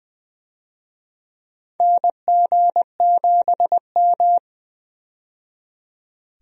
Morse code, heard as a keyed tone, about 20 wpm, 700 Hz. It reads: NG7M
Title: USA Call Signs - 4 Characters - 20wpm